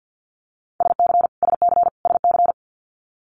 Morse code, heard as a keyed tone, about 50 wpm, 700 Hz. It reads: SC SC SC
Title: Multi-letter Phonemes Repeated 3x 50wpm